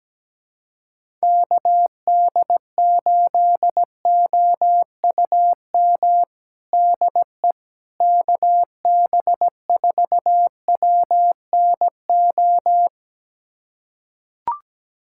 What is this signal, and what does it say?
Morse code, heard as a keyed tone, about 17 wpm, 700 Hz. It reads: KD8OUM DE KB4WNO
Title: USA Callsign DE Callsign 17wpm